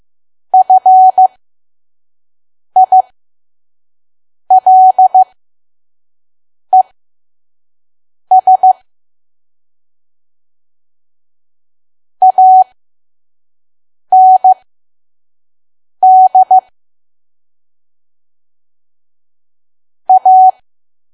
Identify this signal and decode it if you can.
Morse code, heard as a keyed tone, about 15 wpm, spaced as for 5 wpm, 750 Hz. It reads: FILES AND A
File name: Archive / 5 WPM files / PM